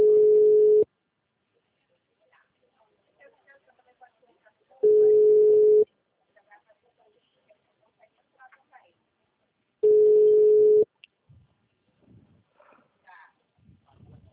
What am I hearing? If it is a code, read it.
Ringback tone (Europe, 425 Hz), with speech.